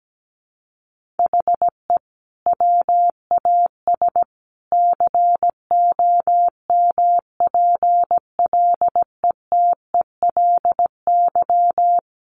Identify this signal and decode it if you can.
Morse code, heard as a keyed tone, about 17 wpm, 700 Hz. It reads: HE WAS COMPLETELY
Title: Sentences from Top 1100 Words 17wpm